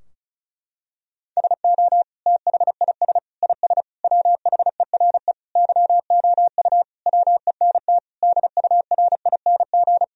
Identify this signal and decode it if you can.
Morse code, heard as a keyed tone, about 34 wpm, 700 Hz. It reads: SO THIS IS WHERE YOU WENT DURING
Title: Sentences from Top 1000 Words - Review Part 2 - 35wpm